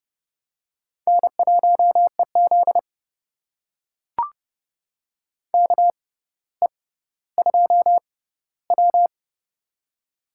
Morse code, heard as a keyed tone, about 30 wpm, spaced as for 15 wpm, 700 Hz. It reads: N1EZ KE2W
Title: USA Call Signs - 4 Characters - 6x Character Spacing - 30wpm